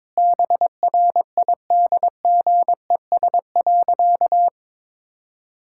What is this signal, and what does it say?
Morse code, heard as a keyed tone, about 21 wpm, 700 Hz. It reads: BRIDGES.
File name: Difficult Sentences with Hints - 22wpm